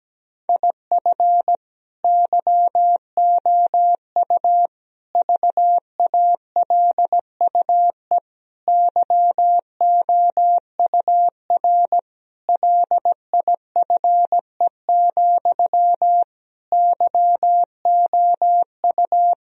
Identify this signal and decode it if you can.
Morse code, heard as a keyed tone, about 17 wpm, 700 Hz. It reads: IF YOU VALUE YOUR LIFE, YOU
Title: Sentences from Top 1000 Words - Review Part 3 - 17wpm